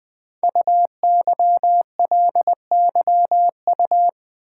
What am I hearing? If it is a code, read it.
Morse code, heard as a keyed tone, about 20 wpm, 700 Hz. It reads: UYLYU
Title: Code Groups - 5 Letters - 20wpm